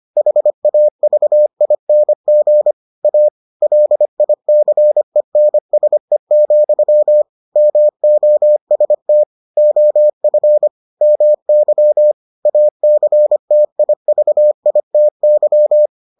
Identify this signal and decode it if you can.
Morse code, heard as a keyed tone, about 25 wpm, 600 Hz. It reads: HAVING A LICENSE, MOST OF MY ACTIVITY